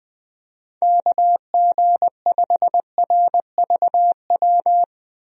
Morse code, heard as a keyed tone, about 20 wpm, 700 Hz. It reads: KG5RVW
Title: USA Call Signs - 6 Characters - 20wpm